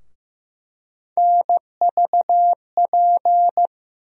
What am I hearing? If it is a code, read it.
Morse code, heard as a keyed tone, about 15 wpm, 700 Hz. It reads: NVP
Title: USA Call Sign Postfixes - 3 Letters 15wpm